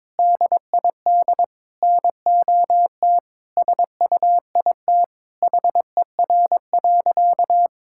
Morse code, heard as a keyed tone, about 22 wpm, 700 Hz. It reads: DID NOT SUIT HER.
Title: Sentences from Top 1600 Words 22wpm